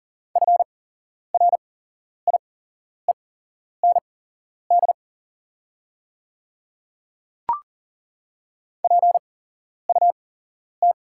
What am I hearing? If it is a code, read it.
Morse code, heard as a keyed tone, about 40 wpm, spaced as for 11 wpm, 690 Hz. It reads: FRIEND PUT
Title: 8x Character Spacing - Top 500 Words - 40wpm